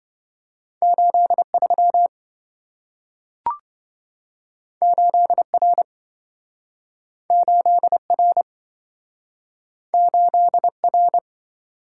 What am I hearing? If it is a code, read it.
Morse code, heard as a keyed tone, about 27 wpm, 700 Hz. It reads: 83 8R 8R 8R